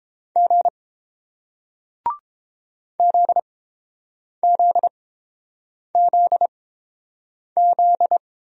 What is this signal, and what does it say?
Morse code, heard as a keyed tone, about 30 wpm, spaced as for 7 wpm, 700 Hz. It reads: G ZZZZ